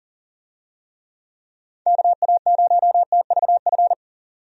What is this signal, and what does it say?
Morse code, heard as a keyed tone, about 40 wpm, 700 Hz. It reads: KA0TVF
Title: USA Call Signs - 6 Characters - 40wpm